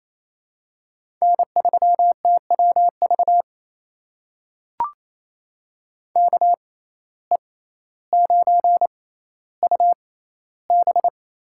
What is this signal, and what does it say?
Morse code, heard as a keyed tone, about 28 wpm, spaced as for 14 wpm, 700 Hz. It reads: N3TWV KE9UB
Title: USA Call Signs - 5 Characters - 6x Character Spacing - 28wpm